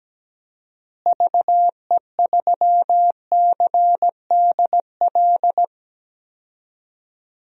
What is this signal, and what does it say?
Morse code, heard as a keyed tone, about 17 wpm, 700 Hz. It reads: VE3CDL
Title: DXCC-100 Call Signs - 6 Characters - Repeated 2x - 17wpm